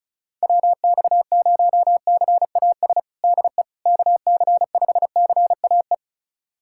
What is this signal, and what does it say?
Morse code, heard as a keyed tone, about 34 wpm, 690 Hz. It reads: WX0CAS DE KC5CAE
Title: USA Callsign DE Callsign - 6 Characters - 35wpm